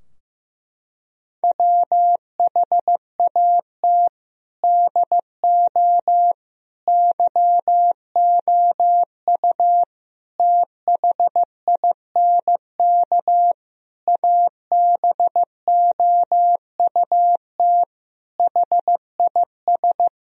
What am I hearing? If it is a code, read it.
Morse code, heard as a keyed tone, about 16 wpm, 700 Hz. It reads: WHAT DO YOU THINK ABOUT HIS
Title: Sentences from Top 1000 Words - Review Part 4 - 15wpm